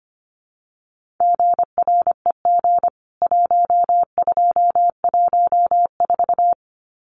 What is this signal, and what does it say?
Morse code, heard as a keyed tone, about 25 wpm, 700 Hz. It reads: GREG 1214
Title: CWT Exchange 25wpm